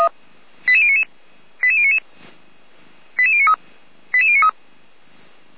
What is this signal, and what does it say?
Five-tone selective call (ZVEI), heard as quick runs of five tones, tones of about 70 ms. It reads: 88099 88099 88093 88093
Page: Server at audio.f85360.feuerwehr-orlamuende.org